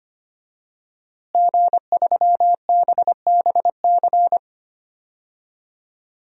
Morse code, heard as a keyed tone, about 25 wpm, 700 Hz. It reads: G3BBC
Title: DXCC-100 Call Signs - 5 Characters - Repeated 3x - 25wpm